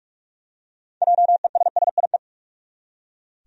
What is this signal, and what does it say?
Morse code, heard as a keyed tone, about 42 wpm, 700 Hz. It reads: JESSIE
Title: USA Common Names - 45wpm